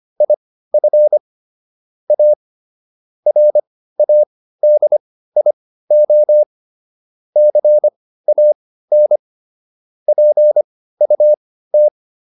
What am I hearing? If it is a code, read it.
Morse code, heard as a keyed tone, about 25 wpm, spaced as for 15 wpm, 600 Hz. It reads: IF A RADIO CAN PUT